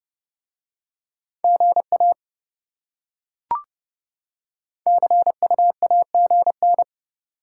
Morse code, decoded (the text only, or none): GA CUAGN